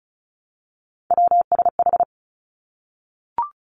Morse code, heard as a keyed tone, about 34 wpm, 700 Hz. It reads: WSH
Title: Three Challenging Letter-Number Combinations 35wpm